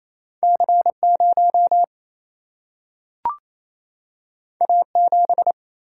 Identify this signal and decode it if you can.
Morse code, heard as a keyed tone, about 28 wpm, 700 Hz. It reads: C0 A7
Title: Single-Letter Single-Number 28wpm